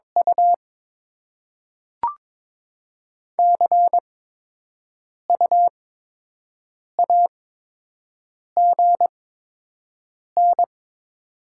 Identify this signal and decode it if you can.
Morse code, heard as a keyed tone, about 22 wpm, spaced as for 6 wpm, 700 Hz. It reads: U CU AGN
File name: Top 20 QSO Elements - 8x Character Spacing - 22wpm